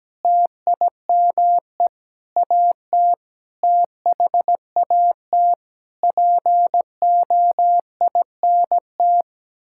Morse code, heard as a keyed tone, about 17 wpm, 700 Hz. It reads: TIME AT THAT POINT
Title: Sentences from Top 1000 Words - Review Part 5 - 17wpm